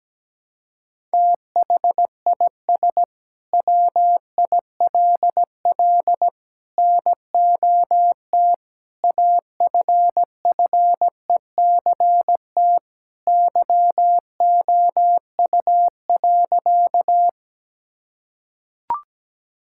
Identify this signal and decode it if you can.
Morse code, heard as a keyed tone, about 17 wpm, 700 Hz. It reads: THIS WILL NOT AFFECT YOU.
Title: Sentences from Top 1000 Words - Review Part 6 - 17wpm